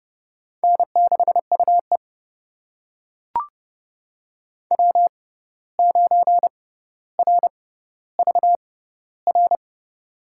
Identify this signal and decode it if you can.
Morse code, heard as a keyed tone, about 30 wpm, spaced as for 14 wpm, 700 Hz. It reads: N6UE W9RVR
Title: USA Call Signs - 6x Character Spacing - 30wpm